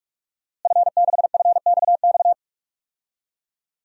Morse code, heard as a keyed tone, about 46 wpm, 700 Hz. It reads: F6FXX
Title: DXCC-100 Call Signs - 5 Characters - Repeated 3x - 45wpm